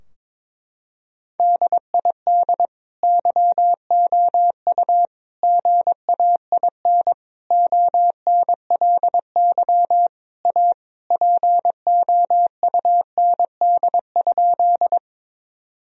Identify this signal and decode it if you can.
Morse code, heard as a keyed tone, about 21 wpm, 700 Hz. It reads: DID YOU GAIN ONLY A POUND?
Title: Sentences from Top 1000 Words - Review Part 2 - 22wpm